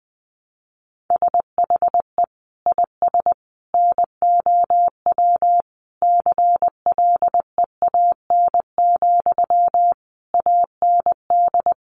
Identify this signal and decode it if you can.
Morse code, heard as a keyed tone, about 20 wpm, 700 Hz. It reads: SHE IS NOW CLEAN, AND
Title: Sentences from Top 1100 Words 20wpm